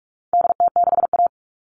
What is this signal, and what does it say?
Morse code, heard as a keyed tone, about 44 wpm, 700 Hz. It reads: DT6A
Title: All Cut Numbers - 4-Digit 45wpm